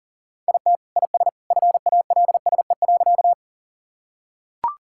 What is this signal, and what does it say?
Morse code, heard as a keyed tone, about 37 wpm, 700 Hz. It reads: IT IS FALSE.
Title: Sentences from Top 1600 Words 40wpm